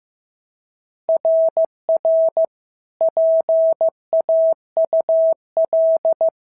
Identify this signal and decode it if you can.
Morse code, heard as a keyed tone, about 14 wpm, 650 Hz. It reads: RR PAUL
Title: SOTA QSO Practice 15wpm